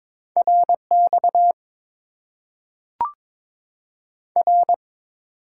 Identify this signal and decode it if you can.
Morse code, heard as a keyed tone, about 22 wpm, 700 Hz. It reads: RX R
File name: Top 90 QSO Elements - 8x Character Spacing - 22wpm